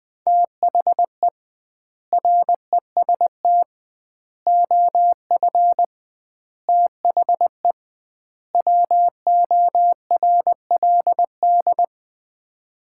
Morse code, heard as a keyed tone, about 20 wpm, 700 Hz. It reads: THE REST OF THE WORLD
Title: 2x Word-Spacing - Sets of 5 Words - 20wpm